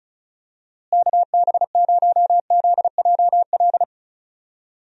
Morse code, heard as a keyed tone, about 36 wpm, 700 Hz. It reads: KB0ZJL
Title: USA Call Signs - 6 Characters - Repeated 2x - 35wpm